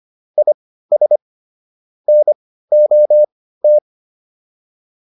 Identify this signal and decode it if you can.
Morse code, heard as a keyed tone, about 25 wpm, spaced as for 15 wpm, 600 Hz. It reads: IS NOT